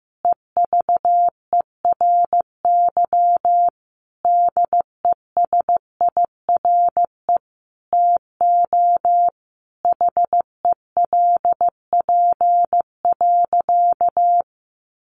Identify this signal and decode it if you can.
Morse code, heard as a keyed tone, about 15 wpm, 700 Hz. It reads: EVERY DESIRE TO HELP.
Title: Sentences from Top 1100 Words 15wpm